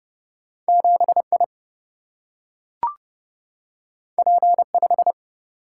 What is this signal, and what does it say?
Morse code, heard as a keyed tone, about 30 wpm, 700 Hz. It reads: 7I P5
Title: Single-Letter Single-Number 30wpm